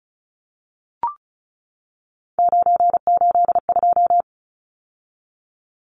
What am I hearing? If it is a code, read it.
Morse code, heard as a keyed tone, about 34 wpm, 700 Hz. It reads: 982
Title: Numbers - 3-Digit 35wpm